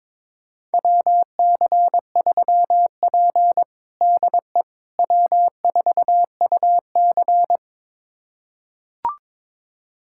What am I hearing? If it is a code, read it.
Morse code, heard as a keyed tone, about 22 wpm, 700 Hz. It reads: WC3P DE W4UC
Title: USA Callsign DE Callsign - 4 Characters - 22wpm